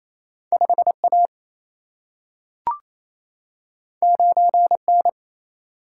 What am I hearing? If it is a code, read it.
Morse code, heard as a keyed tone, about 28 wpm, 700 Hz. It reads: 5 A 9 N
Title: Single-Letter Single-Number 28wpm